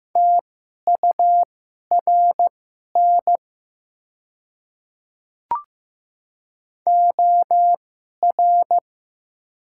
Morse code, heard as a keyed tone, about 15 wpm, spaced as for 11 wpm, 700 Hz. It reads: TURN OR
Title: 2x Character Spacing - Top 200 Words - 15wpm